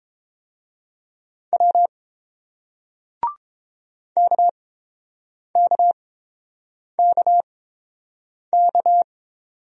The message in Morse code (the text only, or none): W KKKK